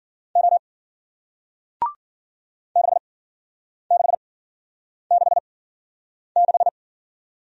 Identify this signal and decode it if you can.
Morse code, heard as a keyed tone, about 55 wpm, spaced as for 9 wpm, 700 Hz. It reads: O 6666